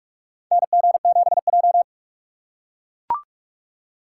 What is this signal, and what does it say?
Morse code, heard as a keyed tone, about 44 wpm, 700 Hz. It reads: NG7J
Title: USA Call Signs - 4 Characters - 6x Character Spacing - 45wpm